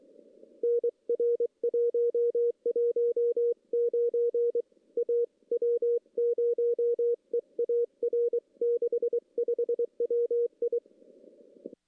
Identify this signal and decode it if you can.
Morse code, heard as a keyed tone, about 23 wpm, 470 Hz. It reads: NR 119 A W0EAR 65 WI
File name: contests / CW-SS-k5tr- -1600